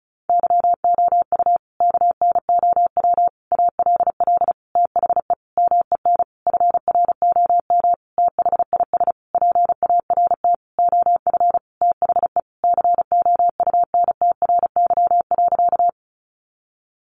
Morse code, heard as a keyed tone, about 34 wpm, 700 Hz. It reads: YOU KNOW ALL THE MEN FROM THIS PART OF THE COUNTRY.
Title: Sentences from Top 300 Words 35wpm